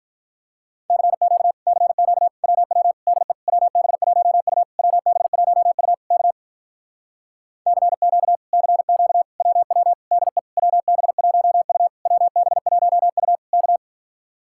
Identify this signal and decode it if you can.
Morse code, heard as a keyed tone, about 47 wpm, 700 Hz. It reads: CQ CQ WW DE WB1U WB1U K CQ CQ WW DE WB1U WB1U K